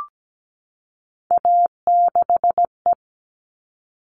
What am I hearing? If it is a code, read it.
Morse code, heard as a keyed tone, about 17 wpm, 700 Hz. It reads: A6E
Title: All Cut Numbers - 3-Digit 17wpm